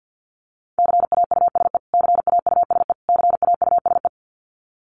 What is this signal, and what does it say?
Morse code, heard as a keyed tone, about 50 wpm, 700 Hz. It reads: CAUSE CAUSE CAUSE
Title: Top 200 Words Repeated 50wpm